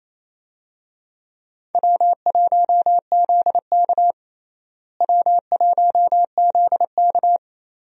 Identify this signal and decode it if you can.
Morse code, heard as a keyed tone, about 28 wpm, 700 Hz. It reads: W1ZK W1ZK
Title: USA Call Signs - 4 Characters - Repeated 3x - 28wpm